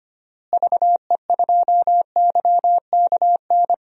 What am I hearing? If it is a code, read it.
Morse code, heard as a keyed tone, about 25 wpm, 700 Hz. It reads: VE2YKN
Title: DXCC-100 Call Signs - 6 Characters - Repeated 3x - 25wpm